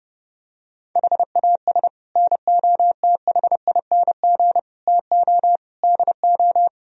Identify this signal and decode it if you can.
Morse code, heard as a keyed tone, about 30 wpm, 700 Hz. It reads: HAS NOTHING TO DO